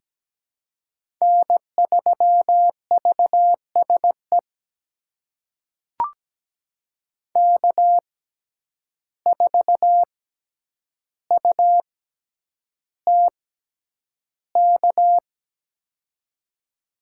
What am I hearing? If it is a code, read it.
Morse code, heard as a keyed tone, about 17 wpm, spaced as for 9 wpm, 700 Hz. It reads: N3VSE K4UTK
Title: USA Call Signs - 5 Characters - 6x Character Spacing - 17wpm